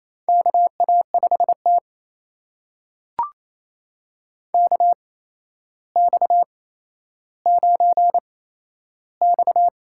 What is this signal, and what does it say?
Morse code, heard as a keyed tone, about 28 wpm, spaced as for 12 wpm, 700 Hz. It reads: KA5T KX9X
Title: USA Call Signs - 4 Characters - 8x Character Spacing - 28wpm